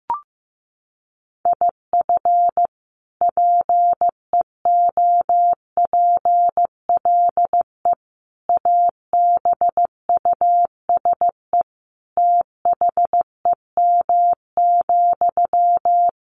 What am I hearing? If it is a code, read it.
Morse code, heard as a keyed tone, about 15 wpm, 700 Hz. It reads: IF PEOPLE ABUSE THEM,
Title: Sentences from Top 1300 Words 15wpm